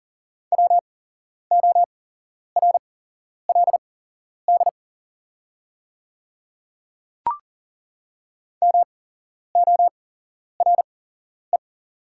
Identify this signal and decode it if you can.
Morse code, heard as a keyed tone, about 40 wpm, spaced as for 11 wpm, 700 Hz. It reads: WORLD MORE
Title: 8x Character Spacing - Top 200 Words - 40wpm